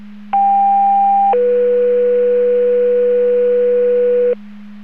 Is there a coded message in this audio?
Yes: two-tone page from 0.3 to 4.3 s, heard as a one-second tone, then a three-second tone, about 800 then 490 Hz, with noise.